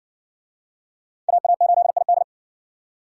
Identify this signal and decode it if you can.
Morse code, heard as a keyed tone, about 51 wpm, 700 Hz. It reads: AA8ID